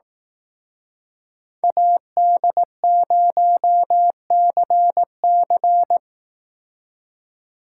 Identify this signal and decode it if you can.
Morse code, heard as a keyed tone, about 18 wpm, 700 Hz. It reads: AD0CC
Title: Speed Racing - USA Call Signs 15wpm